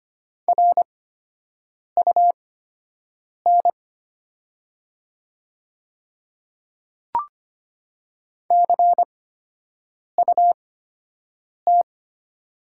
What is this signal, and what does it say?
Morse code, heard as a keyed tone, about 25 wpm, spaced as for 7 wpm, 700 Hz. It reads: RUN CUT
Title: 8x Character Spacing - Top 500 Words - 25wpm